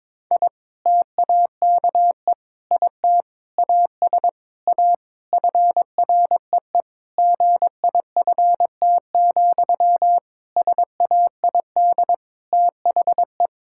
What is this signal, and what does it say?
Morse code, heard as a keyed tone, about 22 wpm, 700 Hz. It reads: I TAKE IT AS A FREE GIFT, SAID THE
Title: Sentences from Top 1000 Words - Review Part 6 - 22wpm